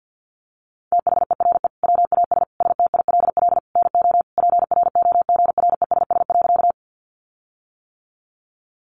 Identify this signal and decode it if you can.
Morse code, heard as a keyed tone, about 48 wpm, 700 Hz. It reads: THERE WAS STILL NO PROGRESS.
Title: Sentences from Top 1400 Words 50wpm